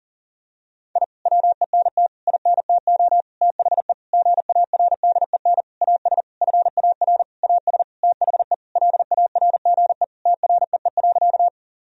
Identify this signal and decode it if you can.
Morse code, heard as a keyed tone, about 40 wpm, 700 Hz. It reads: I WENT INTO THE GARDEN AS FAR AS THE LARGE TREE.
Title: Sentences from Top 1000 Words - Review Part 4 - 40wpm